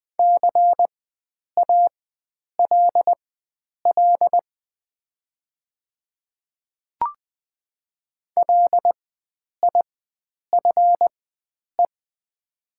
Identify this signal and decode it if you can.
Morse code, heard as a keyed tone, about 20 wpm, spaced as for 9 wpm, 700 Hz. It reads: CALL LIFE
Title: 4x Character Spacing - Top 300 Words - 20wpm